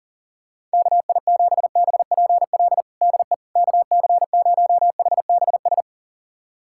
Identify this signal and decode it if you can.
Morse code, heard as a keyed tone, about 40 wpm, 700 Hz. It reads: KI7BPL DE KC0HBS
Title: USA Callsign DE Callsign - 6 Characters - 40wpm